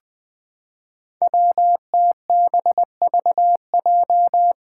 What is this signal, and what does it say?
Morse code, heard as a keyed tone, about 20 wpm, 700 Hz. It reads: WTBVJ
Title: Five Challenging Letter-Number Combinations 20wpm